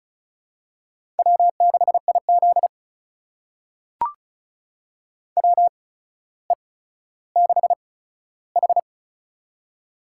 Morse code, heard as a keyed tone, about 34 wpm, spaced as for 15 wpm, 700 Hz. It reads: W6IZ WE6H
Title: USA Call Signs - 4 Characters - 8x Character Spacing - 35wpm